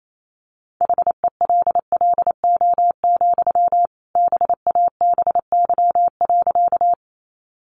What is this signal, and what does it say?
Morse code, heard as a keyed tone, about 28 wpm, 700 Hz. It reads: HELLO, BABY.
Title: Sentences from Top 1300 Words 28wpm